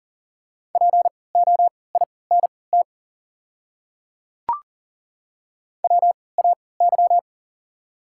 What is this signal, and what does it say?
Morse code, heard as a keyed tone, about 40 wpm, spaced as for 23 wpm, 690 Hz. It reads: POINT WAY
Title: 3x Character Spacing - Top 300 Words - 40wpm